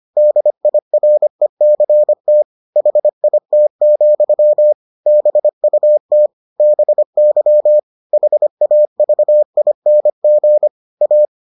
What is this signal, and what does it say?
Morse code, heard as a keyed tone, about 25 wpm, 600 Hz. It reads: DIRECT HIT, BUT BY HAVING A